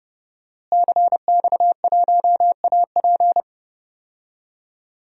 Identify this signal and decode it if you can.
Morse code, heard as a keyed tone, about 30 wpm, 700 Hz. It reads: CX1AP
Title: DXCC-100 Call Signs - 5 Characters - Repeated 2x - 30wpm